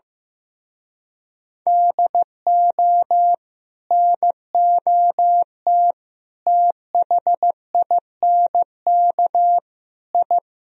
Morse code, heard as a keyed tone, about 16 wpm, 700 Hz. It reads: DO NOT THINK I
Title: Speed-Racing - Sets of 4 Words 15wpm